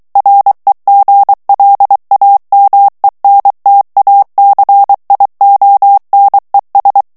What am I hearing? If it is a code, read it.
Morse code, heard as a keyed tone, about 23 wpm, 800 Hz. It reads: REGLAMENTACIONES
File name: morse mp3 / ESPAGNOL TEXTE EN CLAIR 2 VITESSE 22 MOTS MINUTES